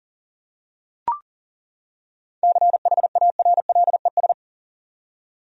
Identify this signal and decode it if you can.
Morse code, heard as a keyed tone, about 40 wpm, 700 Hz. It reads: CHARLES
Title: USA Common Names - 40wpm